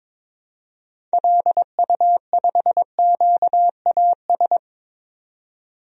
Morse code, heard as a keyed tone, about 22 wpm, 700 Hz. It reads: LU5QAS
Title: DXCC-100 Call Signs - 6 Characters - 22wpm